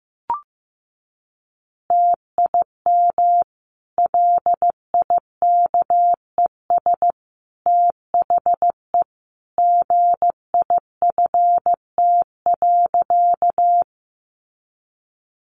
Morse code, heard as a keyed tone, about 15 wpm, 700 Hz. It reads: TIM LIKES THE GIFT.
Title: Easy Phrases 15wpm